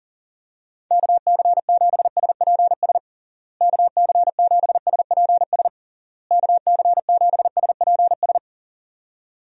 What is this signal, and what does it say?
Morse code, heard as a keyed tone, about 40 wpm, 700 Hz. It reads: KC7SPS KC7SPS KC7SPS
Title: USA Call Signs - 6 Characters - Repeated 3x - 40wpm